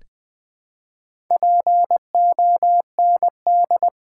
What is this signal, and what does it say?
Morse code, heard as a keyed tone, about 20 wpm, 700 Hz. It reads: POND
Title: Build A Word - 1 Letter At A Time - 4-Letter Words - 20wpm